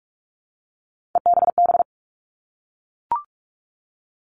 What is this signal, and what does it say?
Morse code, heard as a keyed tone, about 44 wpm, 700 Hz. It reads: EBB
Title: All Cut Numbers - 3-Digit 45wpm